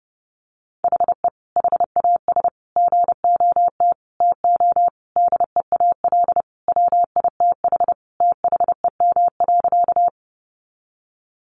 Morse code, heard as a keyed tone, about 30 wpm, 700 Hz. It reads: HE HAS GOT TO DEAL WITH THEM.
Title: Sentences from Top 800 Words 30wpm